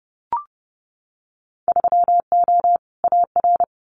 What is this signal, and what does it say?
Morse code, heard as a keyed tone, about 30 wpm, 700 Hz. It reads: 3O AR
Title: Winter Field Day Exchanges Encore 30wpm